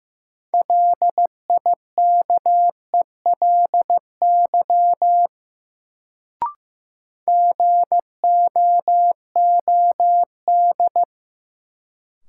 Morse code, heard as a keyed tone, about 15 wpm, 700 Hz. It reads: LIKELY GOOD
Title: Sets of 2 Words - New Word Drill - 15wpm